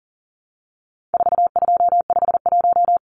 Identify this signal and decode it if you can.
Morse code, heard as a keyed tone, about 40 wpm, 700 Hz. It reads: 4251
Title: Numbers - 4-Digit 40wpm